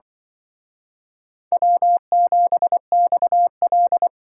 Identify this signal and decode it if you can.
Morse code, heard as a keyed tone, about 24 wpm, 700 Hz. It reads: W7XL
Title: Speed Racing - USA Call Signs 20wpm